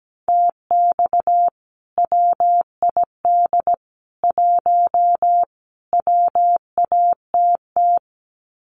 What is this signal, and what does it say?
Morse code, heard as a keyed tone, about 17 wpm, 700 Hz. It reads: TX WID 1 WATT
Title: Common QSO Phrases 17wpm